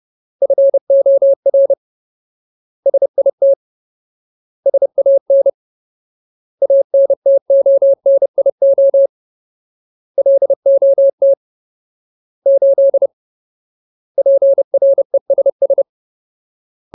Morse code, heard as a keyed tone, about 30 wpm, 550 Hz. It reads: FOR SIT SAN ANTONIO LOT 8 PRESS
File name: Ultimate Word List / 30WPM Ultimate